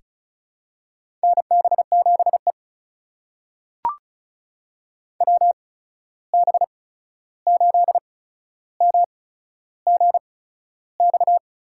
Morse code, heard as a keyed tone, about 36 wpm, spaced as for 13 wpm, 700 Hz. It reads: NB7E WB8MGX
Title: USA Call Signs - 8x Character Spacing - 35wpm